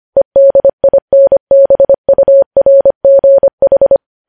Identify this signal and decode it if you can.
Morse code, heard as a keyed tone, about 25 wpm, 550 Hz. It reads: EDINBURGH
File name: podcast / News headlines-1x-25x25